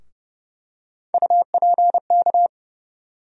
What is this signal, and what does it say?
Morse code, heard as a keyed tone, about 30 wpm, 700 Hz. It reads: UPK
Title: USA Call Sign Postfixes - 3 Letters 30wpm